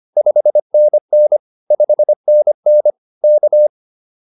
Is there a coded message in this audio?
Yes: Morse code from 0.2 to 3.7 s, heard as a keyed tone, about 25 wpm, 600 Hz. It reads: 5NN 5NN K